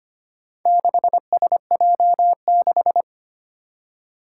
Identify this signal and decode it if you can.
Morse code, heard as a keyed tone, about 25 wpm, 700 Hz. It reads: 6SJ6
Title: Four Challenging Letter-Number Combinations 25wpm